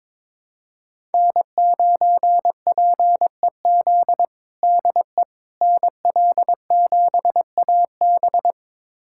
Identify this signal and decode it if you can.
Morse code, heard as a keyed tone, about 22 wpm, 700 Hz. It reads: N9PEZ DE NL7AB
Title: USA Callsign DE Callsign - 5 Characters - 22wpm